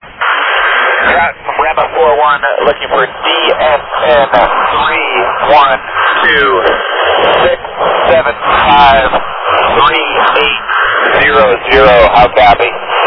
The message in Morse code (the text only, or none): none